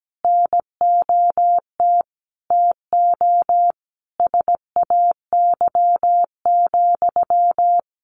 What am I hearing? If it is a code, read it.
Morse code, heard as a keyed tone, about 17 wpm, 700 Hz. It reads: NOT TO SAY,
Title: Sentences from Top 1400 Words 17wpm